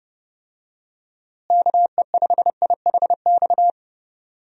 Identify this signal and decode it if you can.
Morse code, heard as a keyed tone, about 30 wpm, 700 Hz. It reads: KE5IHX
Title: USA Call Signs - 6 Characters - Repeated 3x - 30wpm